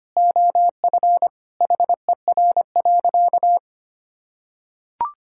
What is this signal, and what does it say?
Morse code, heard as a keyed tone, about 25 wpm, 700 Hz. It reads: OF HER.
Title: Sentences from Top 1000 Words - Review Part 5 - 25wpm